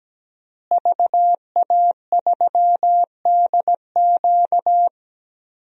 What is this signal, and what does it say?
Morse code, heard as a keyed tone, about 17 wpm, 700 Hz. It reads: VA3DQ
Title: DXCC-100 Call Signs - 5 Characters - Repeated 3x - 17wpm